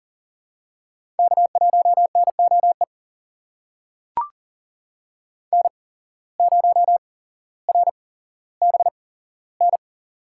Morse code, heard as a keyed tone, about 38 wpm, spaced as for 17 wpm, 700 Hz. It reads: K1NOE N0RBN
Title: USA Call Signs - 5 Characters - 8x Character Spacing - 40wpm